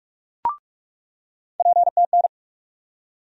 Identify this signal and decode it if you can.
Morse code, heard as a keyed tone, about 46 wpm, 700 Hz. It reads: PTN
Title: USA Call Sign Postfixes - 3 Letters 45wpm